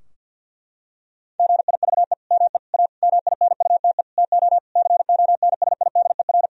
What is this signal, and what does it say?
Morse code, heard as a keyed tone, about 50 wpm, 700 Hz. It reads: GIVE ME A MINUTE TO CONSIDER
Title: Sentences from Top 1000 Words - Review Part 4 - 50wpm